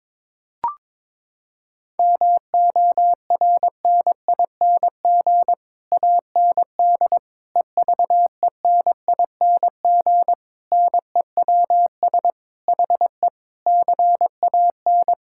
Morse code, heard as a keyed tone, about 22 wpm, 700 Hz. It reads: MORNING AND EVENING NEWS HE CAN
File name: Sets of 9 Words - 22wpm